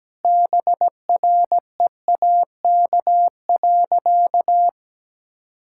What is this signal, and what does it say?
Morse code, heard as a keyed tone, about 17 wpm, 700 Hz. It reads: BREAK.
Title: Sentences from Top 1000 Words - Review Part 6 - 17wpm